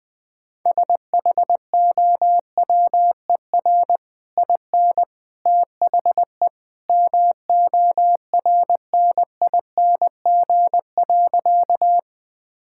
Morse code, heard as a keyed tone, about 20 wpm, 700 Hz. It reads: SHOWER IN THE MORNING.
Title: Easy Sentences with Hints - 20wpm